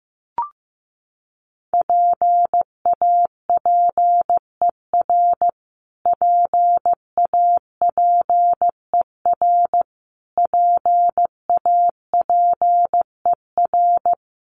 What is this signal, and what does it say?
Morse code, heard as a keyed tone, about 15 wpm, 700 Hz. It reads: PAPER PAPER PAPER
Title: Top 500 Words Repeated 15wpm